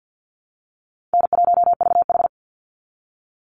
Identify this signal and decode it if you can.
Morse code, heard as a keyed tone, about 50 wpm, 700 Hz. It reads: N1VH
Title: Call Signs Repeated 2x - 50wpm